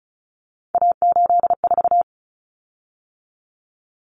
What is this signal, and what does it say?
Morse code, heard as a keyed tone, about 34 wpm, 700 Hz. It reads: A84
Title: Common Cut Numbers - 3-Digit 35wpm